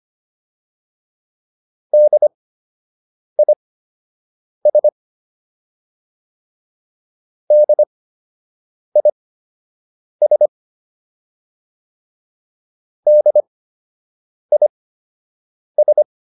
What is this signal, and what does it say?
Morse code, heard as a keyed tone, about 25 wpm, spaced as for 7 wpm, 600 Hz. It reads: DIS DIS DIS